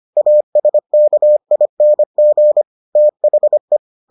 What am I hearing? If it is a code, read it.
Morse code, heard as a keyed tone, about 25 wpm, 600 Hz. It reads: ASKING THE